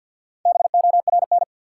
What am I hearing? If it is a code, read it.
Morse code, heard as a keyed tone, about 50 wpm, 700 Hz. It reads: BORN
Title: Sentences from Top 1000 Words - New Word Drill - 50wpm